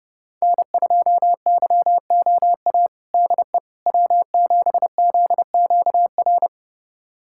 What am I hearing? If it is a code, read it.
Morse code, heard as a keyed tone, about 29 wpm, 700 Hz. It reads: N2YOA DE W7ZQR
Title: USA Callsign DE Callsign - 5 Characters - 30wpm